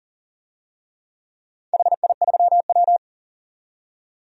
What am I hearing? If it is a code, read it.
Morse code, heard as a keyed tone, about 40 wpm, 700 Hz. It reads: HI3W
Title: DXCC-100 Call Signs - 4 Characters - 40wpm